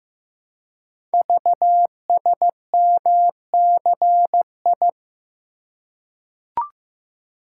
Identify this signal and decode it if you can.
Morse code, heard as a keyed tone, about 15 wpm, 700 Hz. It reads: VSMCI
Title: Code Groups - 5 Letter:Numbers - 15wpm